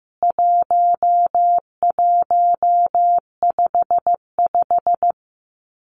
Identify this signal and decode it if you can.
Morse code, heard as a keyed tone, about 15 wpm, 700 Hz. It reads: 1155
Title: CWT Exchange 15wpm